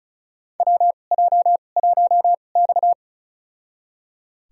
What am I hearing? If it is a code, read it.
Morse code, heard as a keyed tone, about 34 wpm, spaced as for 25 wpm, 700 Hz. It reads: WJ1X
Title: USA Call Signs - 4 Characters - 2x Character Spacing - 35wpm